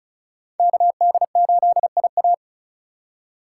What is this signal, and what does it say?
Morse code, heard as a keyed tone, about 45 wpm, 700 Hz. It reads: KD8IA